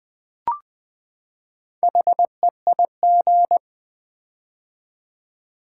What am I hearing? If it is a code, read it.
Morse code, heard as a keyed tone, about 20 wpm, 700 Hz. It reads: HEIG
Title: Build A Word - 4 Letters At A Time - 8-Letter Words - 20wpm